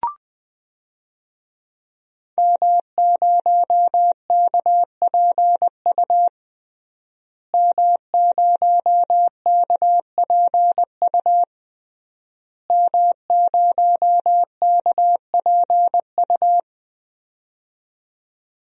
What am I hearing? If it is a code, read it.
Morse code, heard as a keyed tone, about 20 wpm, 700 Hz. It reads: M0KPU M0KPU M0KPU
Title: DXCC-100 Call Signs - 5 Characters - Repeated 3x - 20wpm